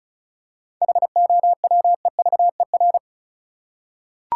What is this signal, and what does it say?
Morse code, heard as a keyed tone, about 36 wpm, 700 Hz. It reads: HOWEVER
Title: Seven Letter Words - 35wpm